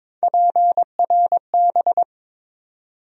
Morse code, heard as a keyed tone, about 22 wpm, 700 Hz. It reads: PRB
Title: USA Call Sign Postfixes - 3 Letters 22wpm